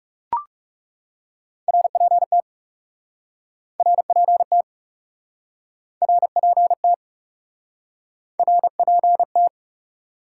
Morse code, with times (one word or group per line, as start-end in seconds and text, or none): 1.7-2.4 RPT
3.8-4.6 RPT
6.0-6.9 RPT
8.4-9.5 RPT